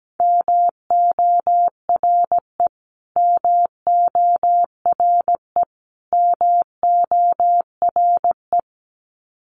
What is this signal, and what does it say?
Morse code, heard as a keyed tone, about 17 wpm, 700 Hz. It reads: MORE MORE MORE
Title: Top 500 Words Repeated 17wpm